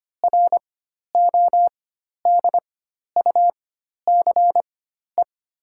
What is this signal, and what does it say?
Morse code, heard as a keyed tone, about 25 wpm, spaced as for 12 wpm, 700 Hz. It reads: RODUCE
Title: 4x Character Spacing - Top 500 Words - 25wpm